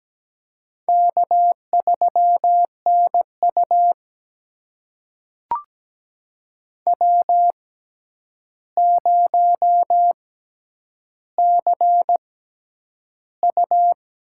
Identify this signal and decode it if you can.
Morse code, heard as a keyed tone, about 17 wpm, spaced as for 9 wpm, 700 Hz. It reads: K3NU W0CU
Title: USA Call Signs - 4 Characters - 6x Character Spacing - 17wpm